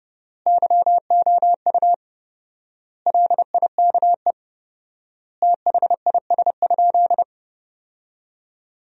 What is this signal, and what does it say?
Morse code, heard as a keyed tone, about 30 wpm, 700 Hz. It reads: YOU LIKE THIS?
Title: 4x Word-Spacing - Sentences from Top 200 Words - 30wpm